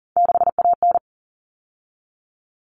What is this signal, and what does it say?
Morse code, heard as a keyed tone, about 40 wpm, 700 Hz. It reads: 6AN
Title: All Cut Numbers - 3-Digit 40wpm